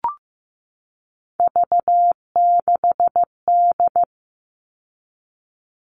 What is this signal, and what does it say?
Morse code, heard as a keyed tone, about 15 wpm, 700 Hz. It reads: V6D
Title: All Cut Numbers - 3-Digit 15wpm